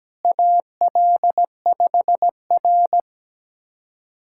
Morse code, heard as a keyed tone, about 17 wpm, 700 Hz. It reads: AL5R
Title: USA Callsign DE Callsign - 4 Characters - 17wpm